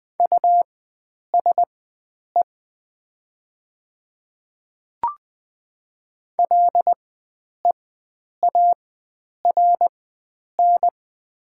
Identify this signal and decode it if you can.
Morse code, heard as a keyed tone, about 20 wpm, spaced as for 9 wpm, 700 Hz. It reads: USE LEARN
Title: 4x Character Spacing - Top 300 Words - 20wpm